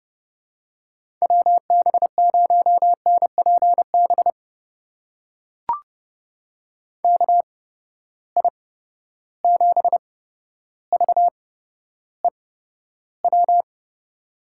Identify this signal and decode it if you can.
Morse code, heard as a keyed tone, about 31 wpm, spaced as for 13 wpm, 700 Hz. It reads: WB0NPB KI7VEW
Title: USA Call Signs - 6 Characters - 8x Character Spacing - 30wpm